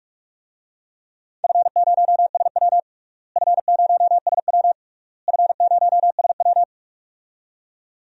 Morse code, heard as a keyed tone, about 46 wpm, 700 Hz. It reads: F0SW F0SW F0SW
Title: DXCC-100 Call Signs - 4 Characters - Repeated 3x - 45wpm